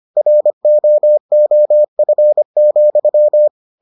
Morse code, heard as a keyed tone, about 25 wpm, 600 Hz. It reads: ROOF,